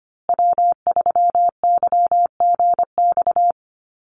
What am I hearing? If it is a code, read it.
Morse code, heard as a keyed tone, about 25 wpm, 700 Hz. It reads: W3YGX
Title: USA Call Signs Repeated 3x 25wpm